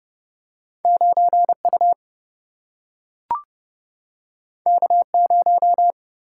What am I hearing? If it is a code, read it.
Morse code, heard as a keyed tone, about 30 wpm, 700 Hz. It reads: 9U K0
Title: Single-Letter Single-Number 30wpm